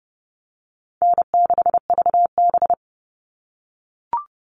Morse code, heard as a keyed tone, about 30 wpm, 700 Hz. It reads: N6VB
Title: All Cut Numbers - 4-Digit 30wpm